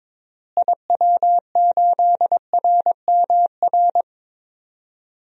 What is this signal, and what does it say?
Morse code, heard as a keyed tone, about 22 wpm, 700 Hz. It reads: IW8RMR
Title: DXCC-100 Call Signs - 6 Characters - Repeated 3x - 22wpm